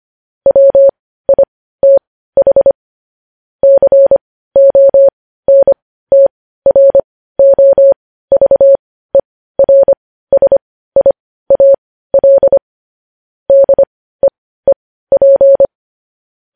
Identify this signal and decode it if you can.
Morse code, heard as a keyed tone, about 25 wpm, spaced as for 15 wpm, 550 Hz. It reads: WITH CONTROVERSIAL DEEP
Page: news-headlines-in-morse-code-at-15-wpm